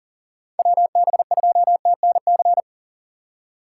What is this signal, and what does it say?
Morse code, heard as a keyed tone, about 40 wpm, 700 Hz. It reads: WB2TNC
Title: USA Call Signs - 6 Characters - 2x Character Spacing - 40wpm